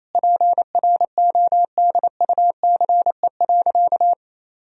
Morse code, heard as a keyed tone, about 28 wpm, 700 Hz. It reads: PRODUCE.
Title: Sentences from Top 1000 Words - Review Part 3 - 28wpm